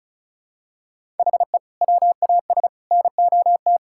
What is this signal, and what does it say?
Morse code, heard as a keyed tone, about 34 wpm, 700 Hz. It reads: HE WAS NOT
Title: Sentences from Top 1000 Words - Review Part 6 - 35wpm